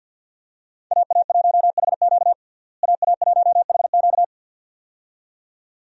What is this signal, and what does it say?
Morse code, heard as a keyed tone, about 50 wpm, 700 Hz. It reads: AA1HQ AA1HQ
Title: USA Call Signs - 5 Characters - Repeated 2x - 50wpm